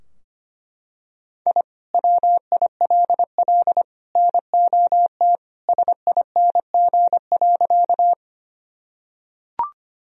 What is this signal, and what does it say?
Morse code, heard as a keyed tone, about 25 wpm, 700 Hz. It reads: I WILL NOT SING.
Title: Sentences from Top 1000 Words - Review Part 2 - 25wpm